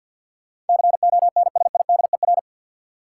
Morse code, heard as a keyed tone, about 53 wpm, 700 Hz. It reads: CONSIDER
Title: Sentences from Top 600 Words - New Word Drill - 50wpm